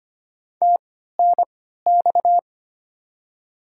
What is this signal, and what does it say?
Morse code, heard as a keyed tone, about 25 wpm, spaced as for 14 wpm, 700 Hz. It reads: TNX
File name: Top 20 QSO Elements - 3x Character Spacing - 25wpm